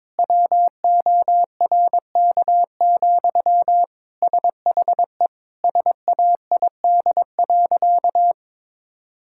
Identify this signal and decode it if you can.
Morse code, heard as a keyed tone, about 22 wpm, 700 Hz. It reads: WORK, SHE SAID.
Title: Sentences from Top 1000 Words - Review Part 6 - 22wpm